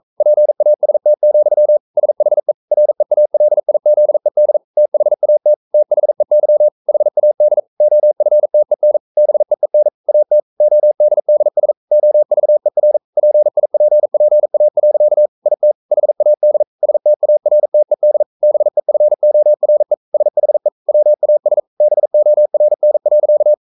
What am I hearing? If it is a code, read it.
Morse code, heard as a keyed tone, about 42 wpm, 600 Hz. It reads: PAST, SHE REALIZED THAT THEY HAD OFTEN BEEN AT ODDS OVER PIPPA. IT HAD STARTED BEFORE SHE WAS BORN.
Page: Romanen Never af Ken Follet på engelsk.